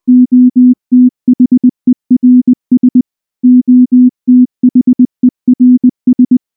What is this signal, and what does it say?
Morse code, heard as a keyed tone, about 19 wpm, 250 Hz. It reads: OTHERS OTHERS
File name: morse / morse-training-20WPM-9